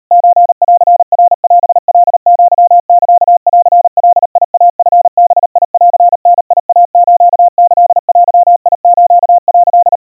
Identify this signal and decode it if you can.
Morse code (Japanese Wabun code), heard as a keyed tone, about 38 wpm, 700 Hz. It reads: ソンナオカアサンガイチバンダイスキデス」
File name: kawa / Morse sample / kamyu 140 cpm 700 Hz Crispy